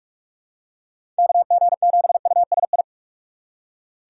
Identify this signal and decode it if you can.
Morse code, heard as a keyed tone, about 44 wpm, 700 Hz. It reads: KG7USI
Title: USA Call Signs - 6 Characters - 6x Character Spacing - 45wpm